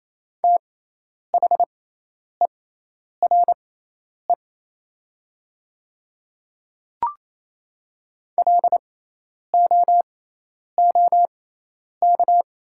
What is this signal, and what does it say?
Morse code, heard as a keyed tone, about 28 wpm, spaced as for 10 wpm, 700 Hz. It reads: THERE LOOK
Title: 6x Character Spacing - Top 200 Words - 28wpm